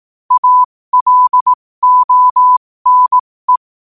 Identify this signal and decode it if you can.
Morse code, heard as a keyed tone, about 18 wpm, spaced as for 15 wpm, 1000 Hz. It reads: ALONE
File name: qotdfiles / 15wpm